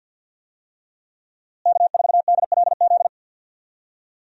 Morse code, heard as a keyed tone, about 52 wpm, 700 Hz. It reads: K4DLZ
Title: USA Call Signs - 5 Characters - Repeated 3x - 50wpm